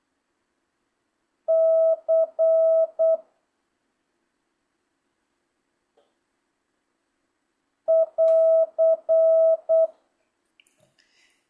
Morse code, heard as a keyed tone, about 8 wpm, 650 Hz. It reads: C <AR>